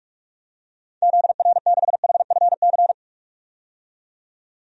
Morse code, heard as a keyed tone, about 41 wpm, 700 Hz. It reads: ZR6HFC
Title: DXCC-100 Call Signs - 6 Characters - 45wpm